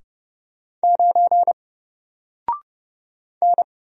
Morse code, heard as a keyed tone, about 30 wpm, 700 Hz. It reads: 9 N
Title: Single Letter-Number Alt 30wpm